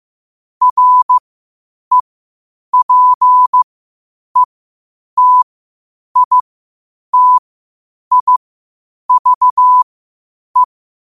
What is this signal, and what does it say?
Morse code, heard as a keyed tone, about 15 wpm, spaced as for 8 wpm, 1000 Hz. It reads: REPETITIVE
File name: qotdfiles / 8wpm